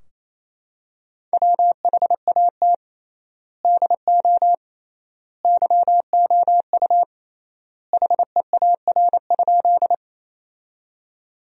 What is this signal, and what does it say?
Morse code, heard as a keyed tone, about 28 wpm, 700 Hz. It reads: WHAT DO YOU HEAR?
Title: 3x Word-Spacing - Sentences from Top 300 Words - 28wpm